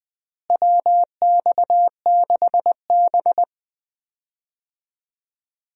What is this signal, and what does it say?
Morse code, heard as a keyed tone, about 20 wpm, 700 Hz. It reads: WX6B
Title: USA Call Signs - 4 Characters - Repeated 2x - 20wpm